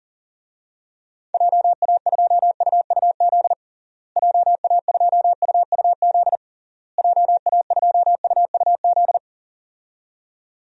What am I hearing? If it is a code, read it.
Morse code, heard as a keyed tone, about 40 wpm, 700 Hz. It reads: JA2UUZ JA2UUZ JA2UUZ
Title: DXCC-100 Call Signs - 6 Characters - Repeated 3x - 40wpm